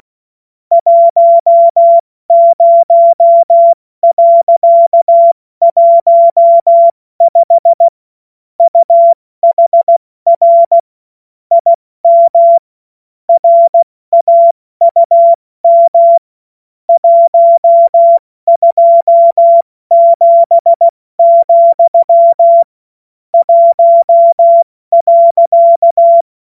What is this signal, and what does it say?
Morse code, heard as a keyed tone, about 16 wpm, 680 Hz. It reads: 10.15 UHR IM RAUM 127, 1.
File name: Morsekurs / Witze / Witz 14.November 2023 70 F80